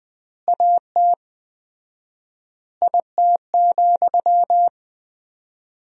Morse code, heard as a keyed tone, about 20 wpm, 700 Hz. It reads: AT IT,
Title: 4x Word-Spacing - Sentences from Top 200 Words - 20wpm